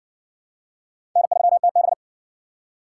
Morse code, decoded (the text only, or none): N3TB